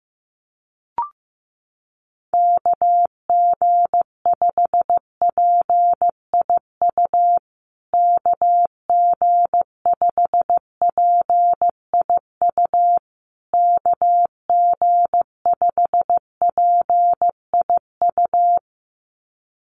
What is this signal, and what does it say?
Morse code, heard as a keyed tone, about 15 wpm, 700 Hz. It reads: KG5PIU KG5PIU KG5PIU
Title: USA Call Signs Repeated 3x 15wpm